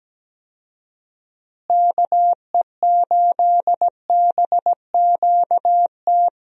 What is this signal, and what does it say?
Morse code, heard as a keyed tone, about 17 wpm, 700 Hz. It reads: KE8BQT
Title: USA Call Signs - 6 Characters - Repeated 2x - 17wpm